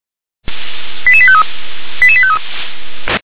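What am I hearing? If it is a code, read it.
Five-tone selective call (ZVEI), heard as quick runs of five tones, tones of about 70 ms. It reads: 88853 88853
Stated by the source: Five-tone ZVEI selective call 88150 received on an SDR.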